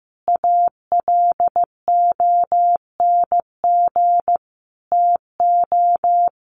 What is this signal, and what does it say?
Morse code, heard as a keyed tone, about 15 wpm, 700 Hz. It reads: ALONG TO
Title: Sentences from Top 1300 Words 15wpm